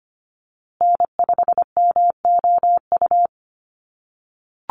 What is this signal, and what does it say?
Morse code, heard as a keyed tone, about 25 wpm, 700 Hz. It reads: N5MOU
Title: USA Call Signs Repeated 3x 25wpm